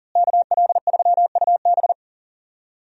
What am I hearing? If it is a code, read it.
Morse code, heard as a keyed tone, about 40 wpm, 700 Hz. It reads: KL3UB
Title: USA Call Signs - 5 Characters - Repeated 2x - 40wpm